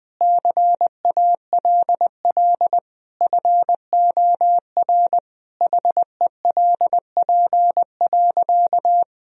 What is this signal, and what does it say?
Morse code, heard as a keyed tone, about 21 wpm, 700 Hz. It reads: CALL FOR HELP.
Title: Sentences from Top 1000 Words - Review Part 4 - 20wpm